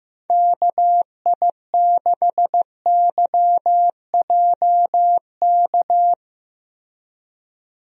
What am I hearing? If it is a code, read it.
Morse code, heard as a keyed tone, about 15 wpm, 700 Hz. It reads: KI6YJK
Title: USA Call Signs - 6 Characters - Repeated 3x - 15wpm